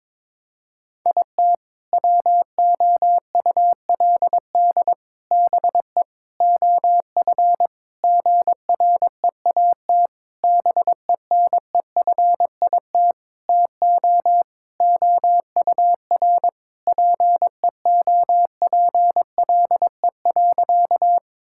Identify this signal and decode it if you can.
Morse code, heard as a keyed tone, about 22 wpm, 700 Hz. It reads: IT WOULD BE OF GREAT BENEFIT TO OUR PEOPLE.
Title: Sentences from Top 1000 Words - Review Part 6 - 22wpm